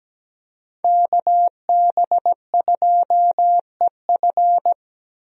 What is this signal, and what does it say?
Morse code, heard as a keyed tone, about 17 wpm, 700 Hz. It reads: KB2EF
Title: USA Call Signs - 5 Characters - Repeated 2x - 17wpm